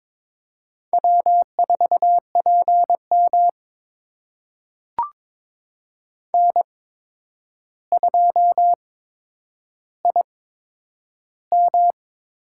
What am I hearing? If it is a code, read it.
Morse code, heard as a keyed tone, about 22 wpm, spaced as for 9 wpm, 700 Hz. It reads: W4PM N2IM
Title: USA Call Signs - 4 Characters - 8x Character Spacing - 22wpm